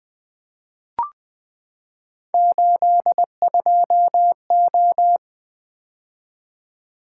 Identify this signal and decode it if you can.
Morse code, heard as a keyed tone, about 20 wpm, 700 Hz. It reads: 82O
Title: Three Letter-Number Combinations - 20wpm